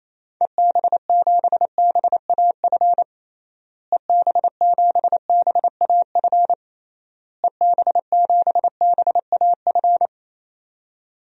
Morse code, heard as a keyed tone, about 28 wpm, 700 Hz. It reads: EB7BAF EB7BAF EB7BAF
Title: DXCC-100 Call Signs - 6 Characters - Repeated 3x - 28wpm